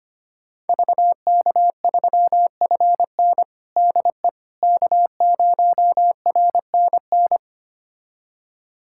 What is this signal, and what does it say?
Morse code, heard as a keyed tone, about 25 wpm, 700 Hz. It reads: VK3FN DE K0RNN
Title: USA Callsign DE Callsign - 5 Characters - 25wpm